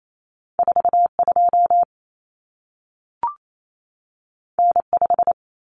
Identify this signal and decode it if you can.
Morse code, heard as a keyed tone, about 28 wpm, 700 Hz. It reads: 42 N5
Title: Common Cut Numbers - 2-Digit 28wpm